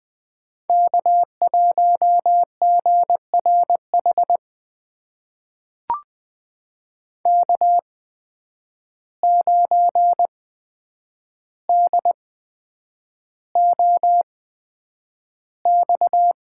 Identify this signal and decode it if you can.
Morse code, heard as a keyed tone, about 20 wpm, spaced as for 9 wpm, 700 Hz. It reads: K1GRH K9DOX
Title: USA Call Signs - 5 Characters - 8x Character Spacing - 20wpm